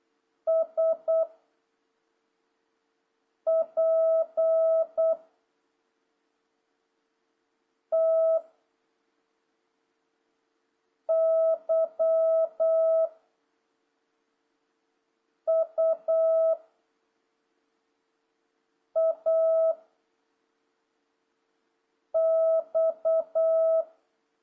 Morse code, heard as a keyed tone, about 8 wpm, 650 Hz. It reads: S P T Y U A X
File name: jh8cbh / 2026cw